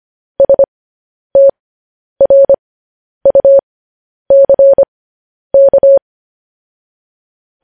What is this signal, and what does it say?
Morse code, heard as a keyed tone, about 25 wpm, spaced as for 10 wpm, 550 Hz. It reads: STRUCK
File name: podcast / News headlines-1x-25x10